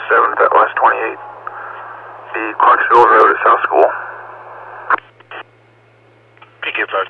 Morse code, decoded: none

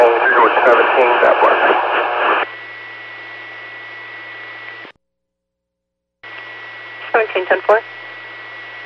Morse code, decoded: none

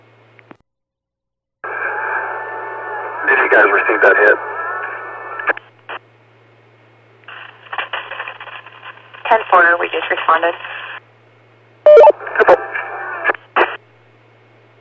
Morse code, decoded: none

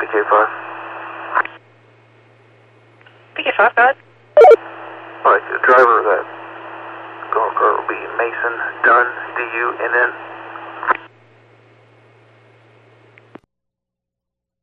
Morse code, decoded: none

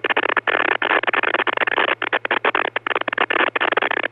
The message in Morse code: none

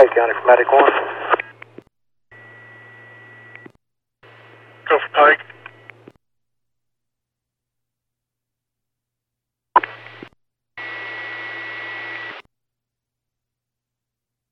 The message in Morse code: none

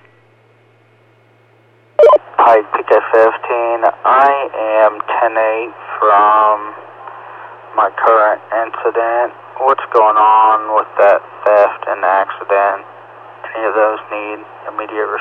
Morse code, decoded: none

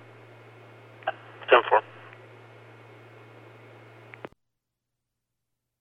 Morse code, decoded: none

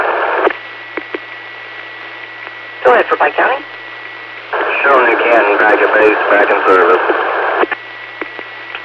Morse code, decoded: none

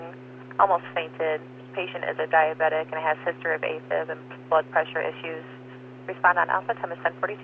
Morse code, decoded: none

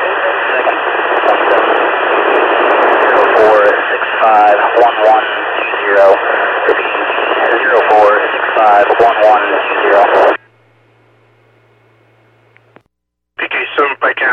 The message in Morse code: none

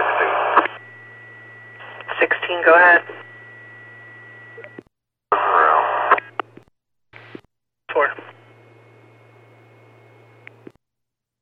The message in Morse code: none

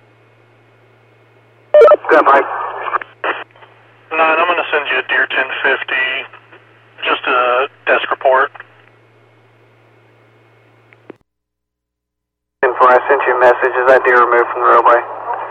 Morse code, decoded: none

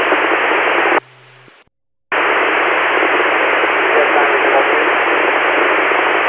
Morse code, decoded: none

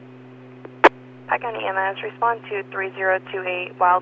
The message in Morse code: none

